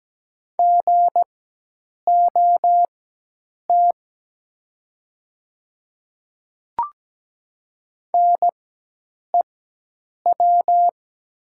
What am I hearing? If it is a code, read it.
Morse code, heard as a keyed tone, about 17 wpm, spaced as for 8 wpm, 700 Hz. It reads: GOT NEW
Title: 4x Character Spacing - Top 300 Words - 17wpm